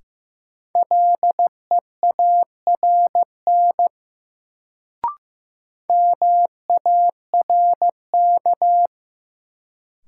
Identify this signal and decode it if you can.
Morse code, heard as a keyed tone, about 15 wpm, 700 Hz. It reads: LEARN MARK
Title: Sentences from Top 300 Words - New Word Drill - 15wpm